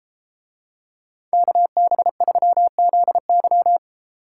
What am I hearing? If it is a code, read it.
Morse code, heard as a keyed tone, about 32 wpm, 700 Hz. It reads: KB3ZY